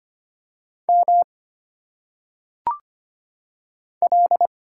Morse code, heard as a keyed tone, about 25 wpm, 700 Hz. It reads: M L
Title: Single Letters - Phonetic Pronunciation 25wpm